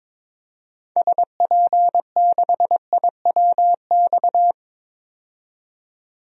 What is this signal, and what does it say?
Morse code, heard as a keyed tone, about 22 wpm, 700 Hz. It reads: SP6IWX
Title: DXCC-100 Call Signs - 6 Characters - 22wpm